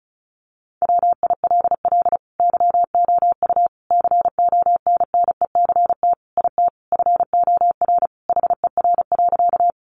Morse code, heard as a keyed tone, about 34 wpm, 700 Hz. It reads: WILL YOU CONNECT IT FOR HER.
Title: Sentences from Top 1500 Words 35wpm